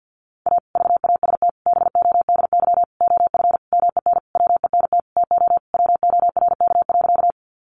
Morse code, heard as a keyed tone, about 50 wpm, 700 Hz. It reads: A VAST BODY OF MEN WENT TO WORK.
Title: Sentences from Top 1300 Words 50wpm